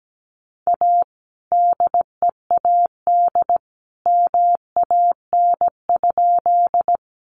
Morse code, heard as a keyed tone, about 17 wpm, 700 Hz. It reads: A DEAD MAN?
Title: Sentences from Top 1500 Words 17wpm